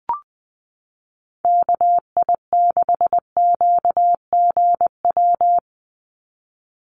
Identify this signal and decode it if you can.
Morse code, heard as a keyed tone, about 20 wpm, 700 Hz. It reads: KI6QGW
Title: USA Call Signs 20wpm